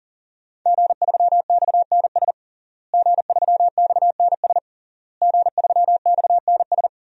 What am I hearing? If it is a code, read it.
Morse code, heard as a keyed tone, about 40 wpm, 700 Hz. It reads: G3XNS G3XNS G3XNS
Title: DXCC-100 Call Signs - 5 Characters - Repeated 3x - 40wpm